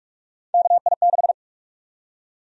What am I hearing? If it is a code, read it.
Morse code, heard as a keyed tone, about 42 wpm, 700 Hz. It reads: KI6
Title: USA Call Sign Prefixes - 45wpm